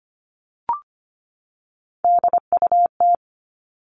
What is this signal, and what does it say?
Morse code, heard as a keyed tone, about 25 wpm, 700 Hz. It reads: DUT
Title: All Cut Numbers - 3-Digit 25wpm